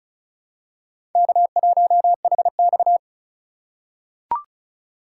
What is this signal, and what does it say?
Morse code, heard as a keyed tone, about 34 wpm, 700 Hz. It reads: K1HX
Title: USA Call Signs - 4 Characters - 4x Character Spacing - 35wpm